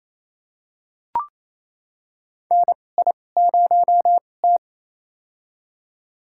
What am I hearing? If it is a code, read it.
Morse code, heard as a keyed tone, about 27 wpm, spaced as for 20 wpm, 700 Hz. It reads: NI0T
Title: USA Call Signs - 4 Characters - 2x Character Spacing - 28wpm